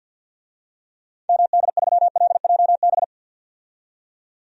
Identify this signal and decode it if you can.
Morse code, heard as a keyed tone, about 50 wpm, 700 Hz. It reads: MD3LJB
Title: DXCC-100 Call Signs - 6 Characters - 50wpm